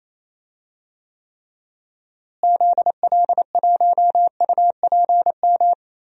Morse code, heard as a keyed tone, about 28 wpm, 700 Hz. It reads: ZL1UPM
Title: DXCC-100 Call Signs - 6 Characters - Repeated 3x - 28wpm